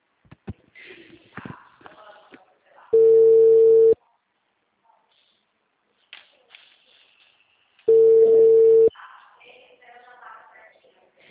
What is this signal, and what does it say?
Ringback tone (China, 450 Hz).
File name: out- -2024- - - .259